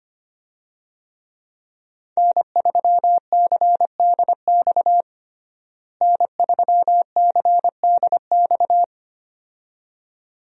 Morse code, heard as a keyed tone, about 25 wpm, 700 Hz. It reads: N3CDX N3CDX
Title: USA Call Signs - 5 Characters - Repeated 2x - 25wpm